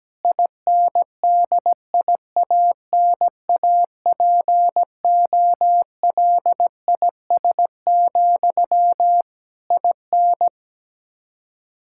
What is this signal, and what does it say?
Morse code, heard as a keyed tone, about 17 wpm, 700 Hz. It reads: INDIANAPOLIS, IN
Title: Practice QSO - Exchange 1 - Beginning - 17wpm